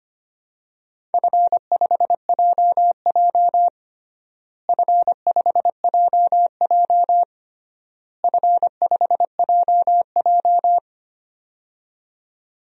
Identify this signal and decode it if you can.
Morse code, heard as a keyed tone, about 25 wpm, 700 Hz. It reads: F5JJ F5JJ F5JJ
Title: DXCC-100 Call Signs - 4 Characters - Repeated 3x - 25wpm